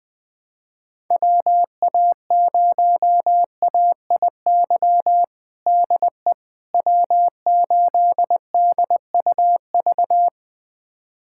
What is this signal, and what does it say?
Morse code, heard as a keyed tone, about 21 wpm, 700 Hz. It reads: WA0AIY DE W8DUV
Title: USA Callsign DE Callsign 20wpm